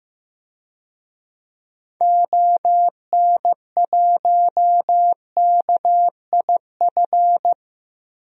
Morse code, heard as a keyed tone, about 15 wpm, 700 Hz. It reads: ON1KIF
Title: DXCC-100 Call Signs - 6 Characters - 15wpm